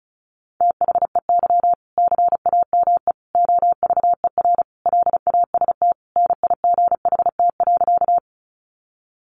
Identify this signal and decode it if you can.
Morse code, heard as a keyed tone, about 34 wpm, 700 Hz. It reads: THEY CAME OVER LAST NIGHT.
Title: Sentences from Top 300 Words 35wpm